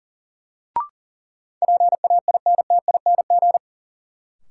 Morse code, heard as a keyed tone, about 40 wpm, 700 Hz. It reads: PAINTING
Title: Sentences from Top 1000 Words - Review Part 6 - New Word Drill - 40wpm